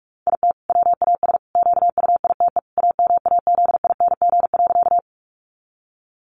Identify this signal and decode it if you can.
Morse code, heard as a keyed tone, about 44 wpm, 700 Hz. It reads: IT WAS QUITE AMAZING.
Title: Sentences from Top 1500 Words 45wpm